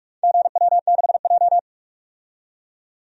Morse code, heard as a keyed tone, about 46 wpm, 700 Hz. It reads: GW6J
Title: DXCC-100 Call Signs - 4 Characters - 45wpm